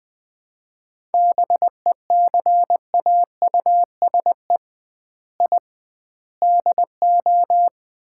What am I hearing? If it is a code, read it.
Morse code, heard as a keyed tone, about 20 wpm, 700 Hz. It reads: BECAUSE I DO
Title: 2x Word-Spacing - Sets of 4 Words - 20wpm